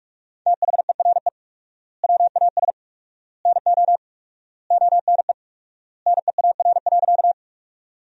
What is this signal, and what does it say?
Morse code, heard as a keyed tone, about 46 wpm, 700 Hz. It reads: THERE WAS NO ONE NEAR.
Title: 4x Word-Spacing - Sentences from Top 200 Words - 45wpm